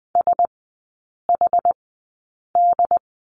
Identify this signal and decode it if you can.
Morse code, heard as a keyed tone, about 20 wpm, spaced as for 8 wpm, 700 Hz. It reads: SHD
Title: 067 Lesson 12e - Letter d - Three Characters 20wpm